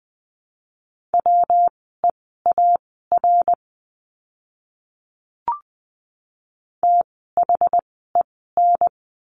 Morse code, heard as a keyed tone, about 20 wpm, spaced as for 14 wpm, 700 Hz. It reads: WEAR THEN
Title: 115 Lesson 19f - Letter w - Words 20wpm